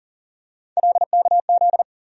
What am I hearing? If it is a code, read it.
Morse code, heard as a keyed tone, about 38 wpm, 700 Hz. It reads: LKZ
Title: USA Call Sign Postfixes - 3 Letters 40wpm